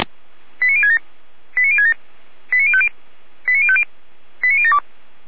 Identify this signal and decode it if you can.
Five-tone selective call (ZVEI), heard as quick runs of five tones, tones of about 70 ms. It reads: 89067 89067 89050 89050 89072